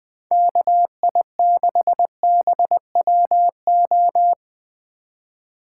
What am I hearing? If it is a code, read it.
Morse code, heard as a keyed tone, about 19 wpm, 700 Hz. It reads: KI6BWO
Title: Speed Racing - USA 2x3 Call Signs 17wpm